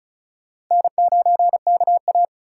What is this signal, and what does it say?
Morse code, heard as a keyed tone, about 34 wpm, 700 Hz. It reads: N9KA
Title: USA Call Signs - 3x Character Spacing - 35wpm